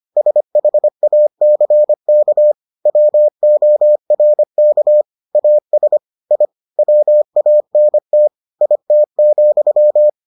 Morse code, heard as a keyed tone, about 25 wpm, 600 Hz. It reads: SHACK WORK AS I WANT IT,